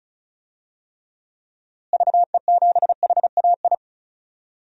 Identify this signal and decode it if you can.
Morse code, heard as a keyed tone, about 36 wpm, 700 Hz. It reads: VE7HAI
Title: DXCC-100 Call Signs - 6 Characters - Repeated 2x - 35wpm